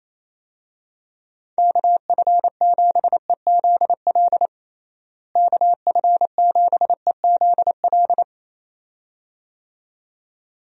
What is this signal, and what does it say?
Morse code, heard as a keyed tone, about 28 wpm, 700 Hz. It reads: KF7EZL KF7EZL
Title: USA Call Signs - 6 Characters - Repeated 2x - 28wpm